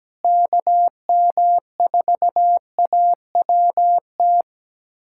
Morse code, heard as a keyed tone, about 17 wpm, 700 Hz. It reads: KM4AWT
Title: USA Callsign DE Callsign - 6 Characters - 17wpm